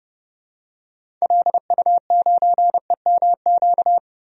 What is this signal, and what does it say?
Morse code, heard as a keyed tone, about 30 wpm, 700 Hz. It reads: LU9EMQ
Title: DXCC-100 Call Signs - 6 Characters - Repeated 2x - 30wpm